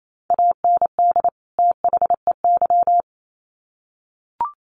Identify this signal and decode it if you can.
Morse code, heard as a keyed tone, about 28 wpm, 700 Hz. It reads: AND THEY
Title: Sets of 2 Words 28wpm